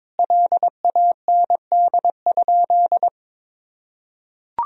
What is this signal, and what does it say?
Morse code, heard as a keyed tone, about 23 wpm, 700 Hz. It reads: LAND?
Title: Sentences from Top 1000 Words - Review Part 3 - 22wpm